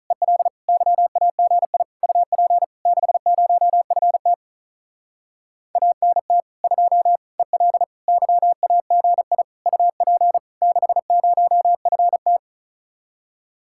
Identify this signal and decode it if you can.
Morse code, heard as a keyed tone, about 45 wpm, 700 Hz. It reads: EL YAGI UP 60FT ANT 2 EL YAGI UP 60FT